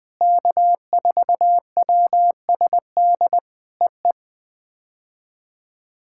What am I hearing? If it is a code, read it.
Morse code, heard as a keyed tone, about 20 wpm, 700 Hz. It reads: K4WSD EE
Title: POTA QSO Practice - 20wpm